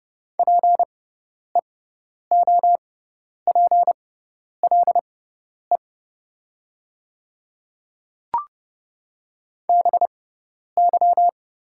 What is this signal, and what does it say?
Morse code, heard as a keyed tone, about 30 wpm, spaced as for 10 wpm, 700 Hz. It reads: PEOPLE BY
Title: 6x Character Spacing - Top 100 Words - 30wpm